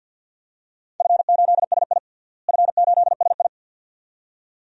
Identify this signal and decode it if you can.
Morse code, heard as a keyed tone, about 53 wpm, 700 Hz. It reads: F8SI F8SI
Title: DXCC-100 Call Signs - 4 Characters - Repeated 2x - 50wpm